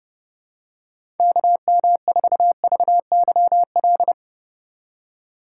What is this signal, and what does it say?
Morse code, heard as a keyed tone, about 32 wpm, 700 Hz. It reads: KM4VYL
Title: USA Call Signs - 6 Characters - 4x Character Spacing - 30wpm